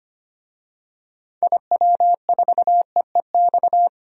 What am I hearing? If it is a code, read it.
Morse code, heard as a keyed tone, about 25 wpm, 700 Hz. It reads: IW4EEX
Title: DXCC-100 Call Signs - 6 Characters - 25wpm